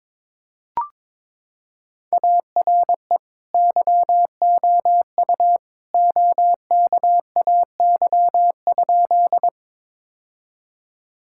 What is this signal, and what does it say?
Morse code, heard as a keyed tone, about 22 wpm, 700 Hz. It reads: ARE YOU OKAY?
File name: practice / 22wpm / Sentences with 2-Word Phrases - 22wpm